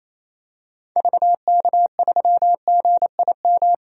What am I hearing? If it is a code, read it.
Morse code, heard as a keyed tone, about 27 wpm, 700 Hz. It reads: VK3GIM
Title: DXCC-100 Call Signs - 6 Characters - 28wpm